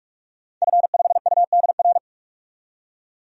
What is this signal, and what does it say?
Morse code, heard as a keyed tone, about 48 wpm, 700 Hz. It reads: F5UDR
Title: DXCC-100 Call Signs - 5 Characters - 45wpm